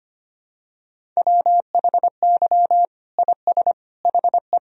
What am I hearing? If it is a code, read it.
Morse code, heard as a keyed tone, about 25 wpm, 700 Hz. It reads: WHY IS HE
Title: Sentences from Top 300 Words and Abbreviations 25wpm